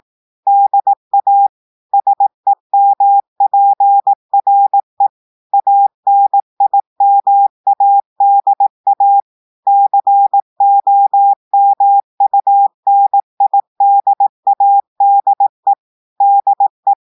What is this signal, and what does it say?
Morse code, heard as a keyed tone, about 18 wpm, 800 Hz. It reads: DA SEMPRE ANIMADA COMUNIDADE DE
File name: audios / portugais / portugais texte en clair 1 vitesse 18 mots minutes